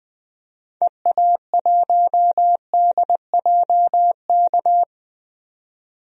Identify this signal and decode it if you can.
Morse code, heard as a keyed tone, about 20 wpm, 700 Hz. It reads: EA1DJK
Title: DXCC-100 Call Signs - 6 Characters - 20wpm